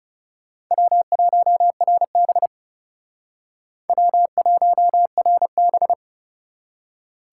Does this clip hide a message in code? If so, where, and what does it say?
Yes: Morse code from 0.7 to 5.9 s, heard as a keyed tone, about 40 wpm, 700 Hz. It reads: W1RB W1RB